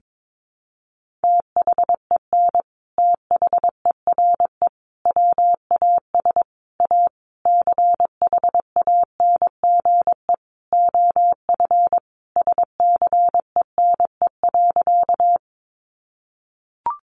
Morse code, heard as a keyed tone, about 22 wpm, 700 Hz. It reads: THEN THERE WAS A CHANGE OF SCENE.
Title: Sentences from Top 1100 Words 22wpm